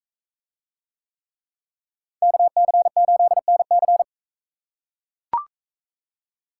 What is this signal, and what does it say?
Morse code, heard as a keyed tone, about 42 wpm, 700 Hz. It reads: KC8NC